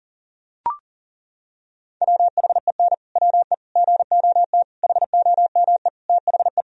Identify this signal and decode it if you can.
Morse code, heard as a keyed tone, about 40 wpm, 700 Hz. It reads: WHEN WE GOT HOME THE
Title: Sentences from Top 500 Words and Abbreviations 40wpm